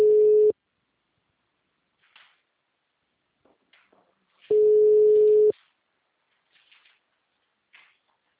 Ringback tone (Europe, 425 Hz).
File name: out- -2022- - - .21